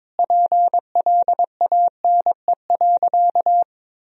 Morse code, heard as a keyed tone, about 22 wpm, 700 Hz. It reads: PLANE.
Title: Sentences from Top 1000 Words - Review Part 3 - 22wpm